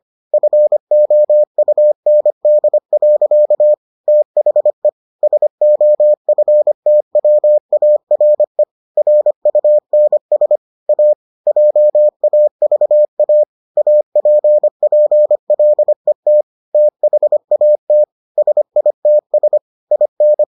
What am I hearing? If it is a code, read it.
Morse code, heard as a keyed tone, about 25 wpm, 600 Hz. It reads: FOUND. THE SOFTWARE RUNS A JAVA APPLET THAT SITS IN